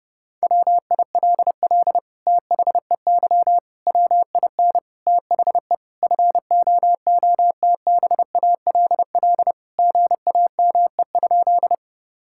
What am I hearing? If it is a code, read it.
Morse code, heard as a keyed tone, about 30 wpm, 700 Hz. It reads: WILL THEY WIN THE FOOTBALL GAME?
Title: Sentences from Top 1600 Words 30wpm